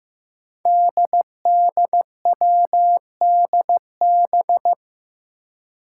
Morse code, heard as a keyed tone, about 15 wpm, 700 Hz. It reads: DDWDB
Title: Five Challenging Letter-Number Combinations 15wpm